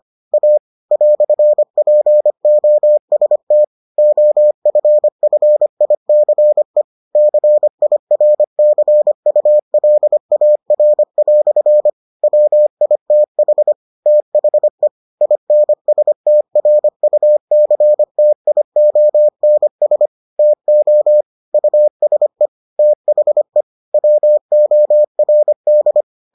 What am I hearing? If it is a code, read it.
Morse code, heard as a keyed tone, about 25 wpm, 600 Hz. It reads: A "POST OFFICE CIRCULAR" WITH THE INSTRUCTIONS TO USE THE WORD